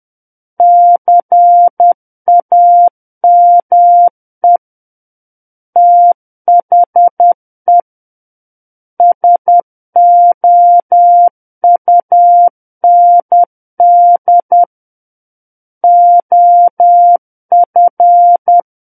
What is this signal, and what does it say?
Morse code, heard as a keyed tone, about 10 wpm, 700 Hz. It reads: CAME THE SOUND OF
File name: Book One / 01-Chapter 1 - 10 WPM